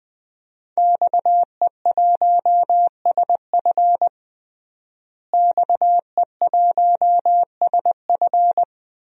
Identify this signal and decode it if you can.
Morse code, heard as a keyed tone, about 21 wpm, 700 Hz. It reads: XE1SF XE1SF
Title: DXCC-100 Call Signs - 5 Characters - Repeated 3x - 20wpm